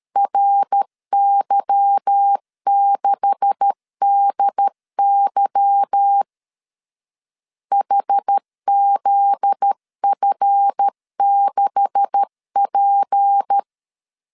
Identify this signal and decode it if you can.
Morse code, heard as a keyed tone, about 13 wpm, 800 Hz. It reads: RY6DY HZF6P
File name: LEZIONI / 9 286 12wpm 2305